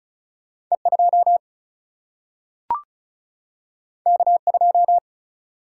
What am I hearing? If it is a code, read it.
Morse code, heard as a keyed tone, about 36 wpm, 700 Hz. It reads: E2 K2
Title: Single-Letter Single-Number 35wpm